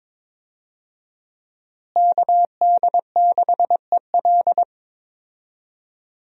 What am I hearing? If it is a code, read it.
Morse code, heard as a keyed tone, about 22 wpm, 700 Hz. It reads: KD6EL
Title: USA Call Signs - 5 Characters - Repeated 2x - 22wpm